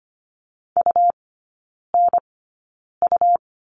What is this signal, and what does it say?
Morse code, heard as a keyed tone, about 25 wpm, spaced as for 9 wpm, 700 Hz. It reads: UNU
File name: Lesson 17e - Letter u - Three Characters 25wpm